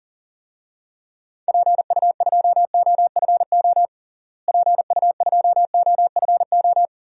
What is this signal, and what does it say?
Morse code, heard as a keyed tone, about 40 wpm, 700 Hz. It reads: PU2OFO PU2OFO
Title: DXCC-100 Call Signs - 6 Characters - Repeated 3x - 40wpm